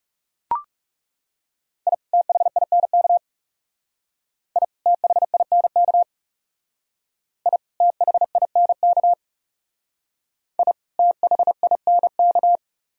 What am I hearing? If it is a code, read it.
Morse code, heard as a keyed tone, about 37 wpm, 700 Hz. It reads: I THINK I THINK I THINK I THINK
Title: Speed-Racing - Sets of 2 Words 30wpm